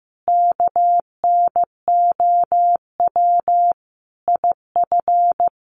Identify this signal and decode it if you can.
Morse code, heard as a keyed tone, about 15 wpm, 700 Hz. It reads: KNOW IF
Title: Sets of 5 Words 15wpm